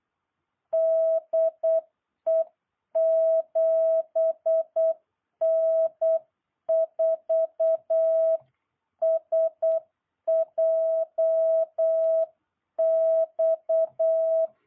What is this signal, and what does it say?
Morse code, heard as a keyed tone, about 8 wpm, 660 Hz. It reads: DE 7N4SJX